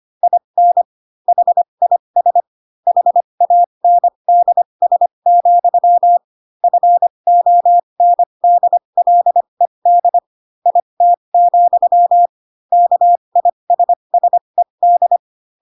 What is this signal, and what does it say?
Morse code, heard as a keyed tone, about 25 wpm, spaced as for 22 wpm, 700 Hz. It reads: IN HIS HANDS, FONDLED IT, KISSED